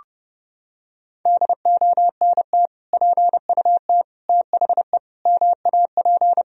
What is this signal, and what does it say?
Morse code, heard as a keyed tone, about 30 wpm, 700 Hz. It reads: DONT PUT THE MAP
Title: Sentences from Top 1000 Words - Review Part 2 - 30wpm